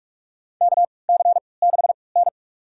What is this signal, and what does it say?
Morse code, heard as a keyed tone, about 42 wpm, spaced as for 25 wpm, 700 Hz. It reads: KC6N
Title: USA Call Signs - 4 Characters - 3x Character Spacing - 45wpm